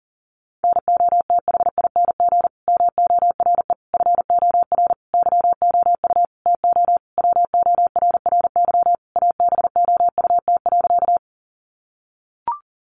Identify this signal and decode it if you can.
Morse code, heard as a keyed tone, about 40 wpm, 700 Hz. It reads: NOTHING MORE FOR YOU TO WORRY ABOUT.
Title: Sentences from Top 1100 Words 40wpm